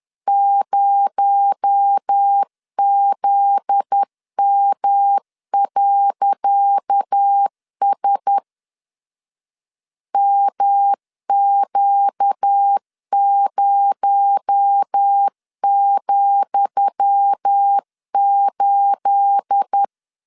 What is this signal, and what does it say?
Morse code, heard as a keyed tone, about 10 wpm, 800 Hz. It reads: 0ZM.S MQ0,8
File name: LEZIONI / 10 SIGNS / 10D COMA 10wpm 3013